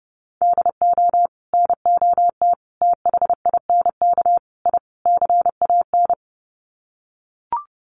Morse code, heard as a keyed tone, about 30 wpm, 700 Hz. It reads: DO NOT THINK I CAN
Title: Sets of 5 Words Encore2 30wpm